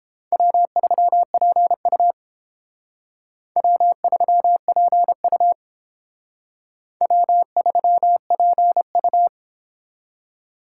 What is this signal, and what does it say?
Morse code, heard as a keyed tone, about 31 wpm, 700 Hz. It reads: W3PU W3PU W3PU